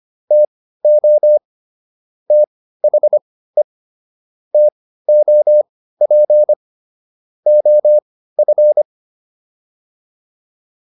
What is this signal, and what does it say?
Morse code, heard as a keyed tone, about 25 wpm, spaced as for 15 wpm, 600 Hz. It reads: TO THE TOP OF